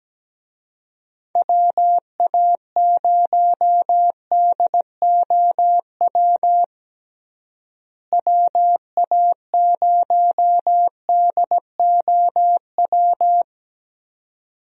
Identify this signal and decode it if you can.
Morse code, heard as a keyed tone, about 17 wpm, 700 Hz. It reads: WA0DOW WA0DOW
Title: USA Call Signs - 6 Characters - Repeated 2x - 17wpm